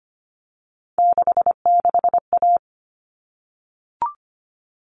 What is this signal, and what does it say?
Morse code, heard as a keyed tone, about 25 wpm, 700 Hz. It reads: 66A
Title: All Cut Numbers - 3-Digit 25wpm